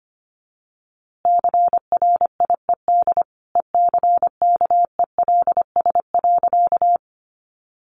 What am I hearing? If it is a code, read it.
Morse code, heard as a keyed tone, about 25 wpm, 700 Hz. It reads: CRIED ECKELS.
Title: A Sound Of Thunder 25wpm